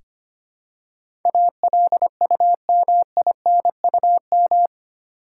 Morse code, heard as a keyed tone, about 25 wpm, 700 Hz. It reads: ALUMINUM
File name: 8 Letter Words - First 3 Letters Given - 25wpm